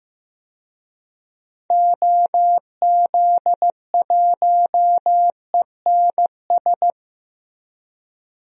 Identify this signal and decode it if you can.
Morse code, heard as a keyed tone, about 15 wpm, 700 Hz. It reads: OZ1ENS
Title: DXCC-100 Call Signs - 6 Characters - Repeated 2x - 15wpm